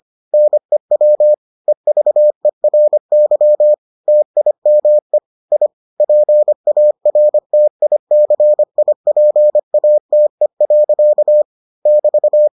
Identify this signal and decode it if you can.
Morse code, heard as a keyed tone, about 25 wpm, 600 Hz. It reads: NEW EVERY TIME I PARTICIPATE. <BT>